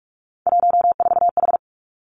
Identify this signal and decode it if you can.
Morse code, heard as a keyed tone, about 46 wpm, 700 Hz. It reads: 14H
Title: Three Challenging Letter-Number Combinations 45wpm